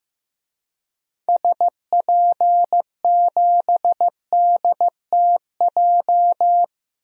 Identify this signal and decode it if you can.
Morse code, heard as a keyed tone, about 15 wpm, 700 Hz. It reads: SP7DTJ
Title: DXCC-100 Call Signs - 6 Characters - 15wpm